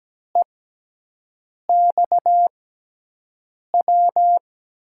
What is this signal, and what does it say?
Morse code, heard as a keyed tone, about 17 wpm, spaced as for 6 wpm, 700 Hz. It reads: EXW
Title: USA Call Signs - 6 Characters - 6x Character Spacing - 17wpm